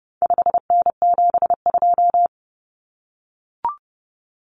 Morse code, heard as a keyed tone, about 30 wpm, 700 Hz. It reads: 5N72
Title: Common Cut Numbers - 4-Digit 30wpm